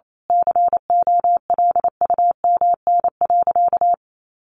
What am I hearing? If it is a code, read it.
Morse code, heard as a keyed tone, about 28 wpm, 700 Hz. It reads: COLUMN.
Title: A Sound Of Thunder 28wpm